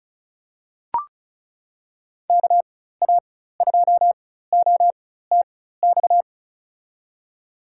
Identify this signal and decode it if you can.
Morse code, heard as a keyed tone, about 36 wpm, spaced as for 16 wpm, 700 Hz. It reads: KA2OTX
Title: USA Call Signs - 6 Characters - 4x Character Spacing - 35wpm